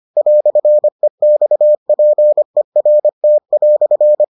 Morse code, heard as a keyed tone, about 25 wpm, 600 Hz. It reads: "EXPERT"